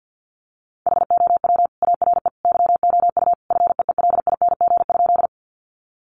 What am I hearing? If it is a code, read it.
Morse code, heard as a keyed tone, about 50 wpm, 700 Hz. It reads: HOW ARE YOU FEELING?
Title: Sentences from Top 1200 Words 50wpm